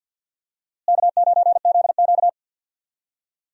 Morse code, heard as a keyed tone, about 49 wpm, 700 Hz. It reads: K9ZQ
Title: USA Call Signs - 4 Characters - Repeated 2x - 50wpm